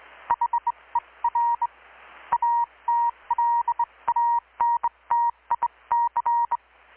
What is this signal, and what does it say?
Morse code, heard as a keyed tone, about 21 wpm, 940 Hz, with radio noise. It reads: HER ATLANTIC
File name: blog / swl / 6-abril-2019-12695-USB-KPS